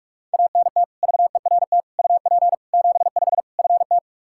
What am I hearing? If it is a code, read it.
Morse code, heard as a keyed tone, about 46 wpm, 700 Hz. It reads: ANT VERT UP 75 FT
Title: Practice QSO - Exchange 2 - Info - 45wpm